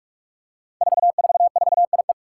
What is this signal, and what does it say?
Morse code, heard as a keyed tone, about 44 wpm, 700 Hz. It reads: 444IE
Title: Five Challenging Letter-Number Combinations 45wpm